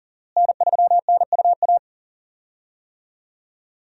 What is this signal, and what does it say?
Morse code, heard as a keyed tone, about 39 wpm, 700 Hz. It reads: N3NUA
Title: USA Call Signs - 5 Characters - Repeated 3x - 40wpm